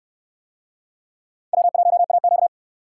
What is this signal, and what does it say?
Morse code, heard as a keyed tone, about 68 wpm, 700 Hz. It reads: W1AQ